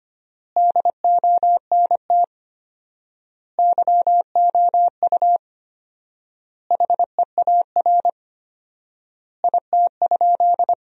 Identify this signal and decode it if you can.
Morse code, heard as a keyed tone, about 25 wpm, 700 Hz. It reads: DONT YOU HEAR IT?
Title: 4x Word-Spacing - Sentences from Top 300 Words - 25wpm